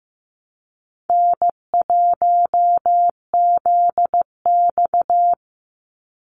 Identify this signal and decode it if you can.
Morse code, heard as a keyed tone, about 15 wpm, 700 Hz. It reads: N1ZX
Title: Easy Call Signs - 15wpm